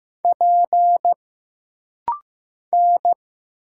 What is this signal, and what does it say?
Morse code, heard as a keyed tone, about 15 wpm, 700 Hz. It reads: P N
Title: Single Letter-Number Alt - Rapid-Fire 15wpm